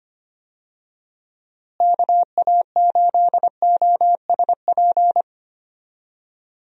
Morse code, heard as a keyed tone, about 25 wpm, 700 Hz. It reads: KA8OSP
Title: USA Call Signs - 6 Characters - Repeated 2x - 25wpm